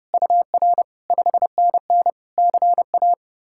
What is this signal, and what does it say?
Morse code, heard as a keyed tone, about 40 wpm, 700 Hz. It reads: UR 5NN CA